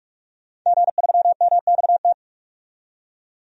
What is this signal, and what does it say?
Morse code, heard as a keyed tone, about 46 wpm, 700 Hz. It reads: G3MXT
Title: DXCC-100 Call Signs - 5 Characters - 45wpm